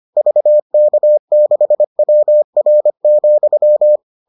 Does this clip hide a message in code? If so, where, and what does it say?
Yes: Morse code from 0.2 to 4.0 s, heard as a keyed tone, about 25 wpm, 600 Hz. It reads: VK6WR,